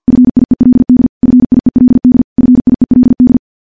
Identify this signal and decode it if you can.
Morse code, heard as a keyed tone, about 50 wpm, 250 Hz. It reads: FIELD FIELD FIELD
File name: morse / morse-training-50WPM